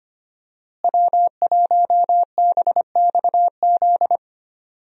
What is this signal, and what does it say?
Morse code, heard as a keyed tone, about 25 wpm, 700 Hz. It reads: W1BXZ
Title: USA Call Signs - 5 Characters - Repeated 2x - 25wpm